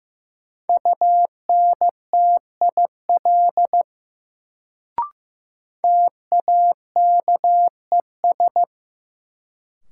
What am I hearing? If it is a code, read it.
Morse code, heard as a keyed tone, about 15 wpm, 700 Hz. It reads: UNTIL TAKES
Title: Sets of 5 Words Encore - New Word Drill - 15wpm